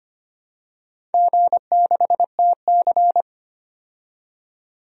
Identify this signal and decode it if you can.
Morse code, heard as a keyed tone, about 25 wpm, 700 Hz. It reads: G6TC
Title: DXCC-100 Call Signs - 4 Characters - 25wpm